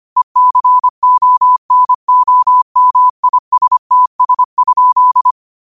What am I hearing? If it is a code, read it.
Morse code, heard as a keyed tone, about 25 wpm, 1000 Hz. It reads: ECONOMISTS?